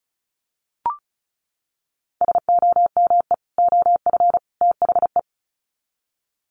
Morse code, heard as a keyed tone, about 34 wpm, 700 Hz. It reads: SOME OF THE
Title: Sets of 3 Words 35wpm